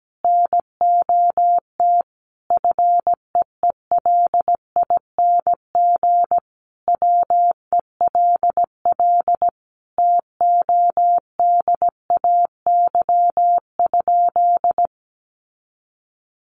Morse code, heard as a keyed tone, about 17 wpm, 700 Hz. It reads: NOT FEELING WELL TODAY?
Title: Sentences from Top 1200 Words 17wpm